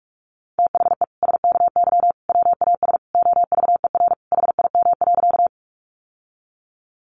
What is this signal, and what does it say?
Morse code, heard as a keyed tone, about 44 wpm, 700 Hz. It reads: THE SKY WAS OVER HIM.
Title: Sentences from Top 1200 Words 45wpm